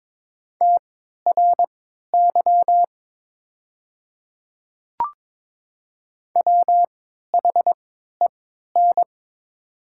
Morse code, heard as a keyed tone, about 22 wpm, spaced as for 12 wpm, 700 Hz. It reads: TRY WHEN
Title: 3x Character Spacing - Top 200 Words - 22wpm